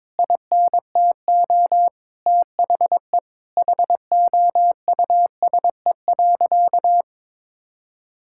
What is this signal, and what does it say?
Morse code, heard as a keyed tone, about 22 wpm, 700 Hz. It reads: INTO THE HOUSE.
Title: Sentences from Top 1000 Words - Review Part 5 - 22wpm